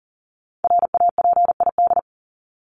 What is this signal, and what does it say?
Morse code, heard as a keyed tone, about 40 wpm, 700 Hz. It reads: RAPID
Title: LY Suffix - Top 100 Words with Base Words 40wpm